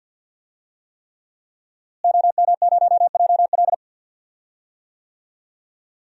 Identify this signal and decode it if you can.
Morse code, heard as a keyed tone, about 53 wpm, 700 Hz. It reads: OM0JL
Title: DXCC-100 Call Signs - 5 Characters - Repeated 3x - 50wpm